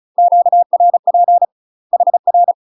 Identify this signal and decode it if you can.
Morse code, heard as a keyed tone, about 34 wpm, 700 Hz. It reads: QRP HR
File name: UK MASTER QSOs TRAINING / 35WPM / UK MASTER QSOs 0001